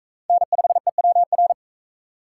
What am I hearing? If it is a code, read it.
Morse code, heard as a keyed tone, about 38 wpm, 700 Hz. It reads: N5EWR